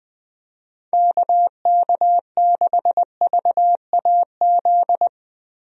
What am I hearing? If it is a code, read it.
Morse code, heard as a keyed tone, about 21 wpm, 700 Hz. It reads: KK6VAZ
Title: USA Call Signs - 6 Characters - Repeated 3x - 20wpm